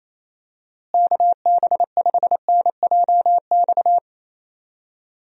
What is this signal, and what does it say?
Morse code, heard as a keyed tone, about 28 wpm, 700 Hz. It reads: KB5NJX
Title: USA Call Signs - 6 Characters - Repeated 2x - 28wpm